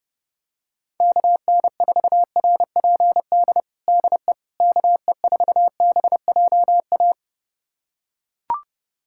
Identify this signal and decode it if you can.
Morse code, heard as a keyed tone, about 30 wpm, 700 Hz. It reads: KN4RPD DE KE4BJA
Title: USA Callsign DE Callsign - 6 Characters - 30wpm